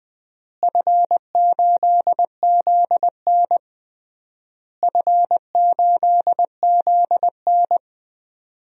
Morse code, heard as a keyed tone, about 19 wpm, 700 Hz. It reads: F8ZN F8ZN
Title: DXCC-100 Call Signs - 4 Characters - Repeated 3x - 20wpm